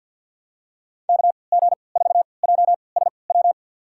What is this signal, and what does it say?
Morse code, heard as a keyed tone, about 53 wpm, spaced as for 29 wpm, 700 Hz. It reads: KG4JSW
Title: USA Call Signs - 6 Characters - 3x Character Spacing - 50wpm